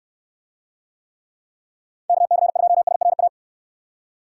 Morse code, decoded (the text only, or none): KC2SRA